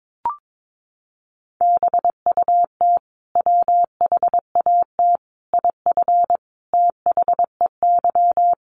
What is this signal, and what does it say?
Morse code, heard as a keyed tone, about 22 wpm, 700 Hz. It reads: BUT WHAT IF THEY
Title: Sentences from Top 1400 Words 22wpm